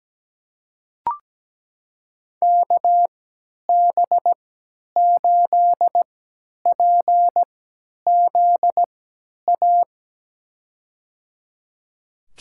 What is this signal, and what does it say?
Morse code, heard as a keyed tone, about 17 wpm, spaced as for 10 wpm, 700 Hz. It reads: KB8PZA
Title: USA Call Signs - 6 Characters - 3x Character Spacing - 17wpm